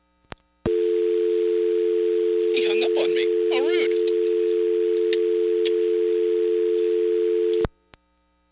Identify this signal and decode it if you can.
Dial tone (North America), 350 + 440 Hz, with speech.
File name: blackratchet / recordings / charles-st-payphone